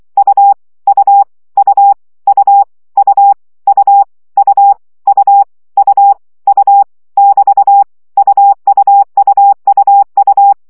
Morse code, heard as a keyed tone, about 24 wpm, 800 Hz. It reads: U U U U U U U U U U <BT> UUUUU
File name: CW gyakorlas u 20 baud